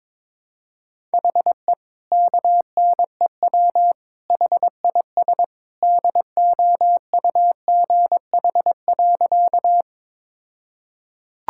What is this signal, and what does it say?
Morse code, heard as a keyed tone, about 22 wpm, 700 Hz. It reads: HE KNEW HIS DOUGH.
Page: morse-code-ninja-practice-22wpm